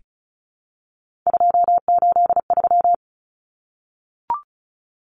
Morse code, heard as a keyed tone, about 34 wpm, 700 Hz. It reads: 283
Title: Common Cut Numbers - 3-Digit 35wpm